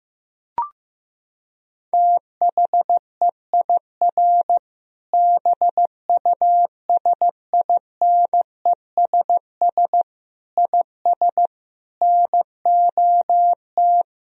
Morse code, heard as a keyed tone, about 15 wpm, 700 Hz. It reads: THEIR BUSINESS IS NOT
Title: Sentences from Top 1000 Words - Review Part 3 - 15wpm